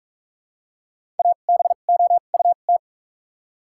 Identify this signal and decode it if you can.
Morse code, heard as a keyed tone, about 48 wpm, spaced as for 33 wpm, 700 Hz. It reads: ABOUT
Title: 2x Character Spacing - Top 100 Words - 45wpm